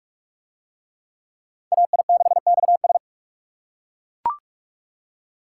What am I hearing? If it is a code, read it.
Morse code, heard as a keyed tone, about 41 wpm, 700 Hz. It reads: AI6XS